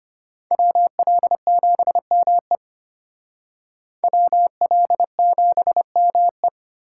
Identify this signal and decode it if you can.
Morse code, heard as a keyed tone, about 32 wpm, 700 Hz. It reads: WL7ME WL7ME